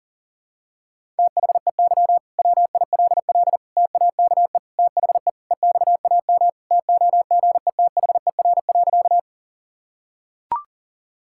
Morse code, heard as a keyed tone, about 40 wpm, 700 Hz. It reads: THEY WILL TAKE THE EXAM TOGETHER.
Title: Sentences from Top 1000 Words - Review Part 5 - 40wpm